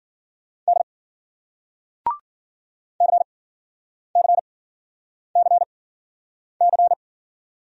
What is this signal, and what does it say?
Morse code, heard as a keyed tone, about 54 wpm, spaced as for 9 wpm, 700 Hz. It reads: D CCCC